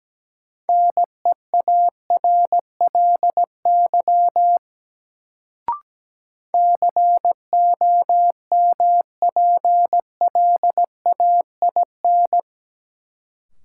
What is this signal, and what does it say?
Morse code, heard as a keyed tone, about 17 wpm, 700 Hz. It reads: NEARLY COMPLAIN
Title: Sentences from Top 1300 Words - New Word Drill Alt - 17wpm